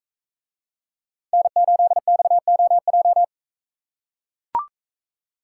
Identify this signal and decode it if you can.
Morse code, heard as a keyed tone, about 42 wpm, 700 Hz. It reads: N8XOJ